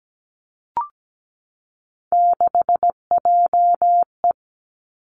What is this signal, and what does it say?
Morse code, heard as a keyed tone, about 17 wpm, 700 Hz. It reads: 6JE
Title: Three Challenging Letter-Number Combinations 17wpm